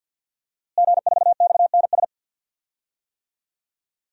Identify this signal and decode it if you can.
Morse code, heard as a keyed tone, about 52 wpm, 700 Hz. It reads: G4XNS
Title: DXCC-100 Call Signs - 5 Characters - Repeated 2x - 50wpm